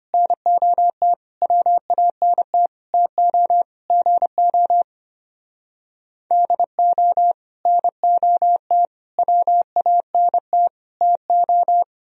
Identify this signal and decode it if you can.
Morse code, heard as a keyed tone, about 31 wpm, 700 Hz. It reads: NOT WANT TO GO DO NOT WANT TO
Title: Speed-Racing - Sets of 5 Words 25wpm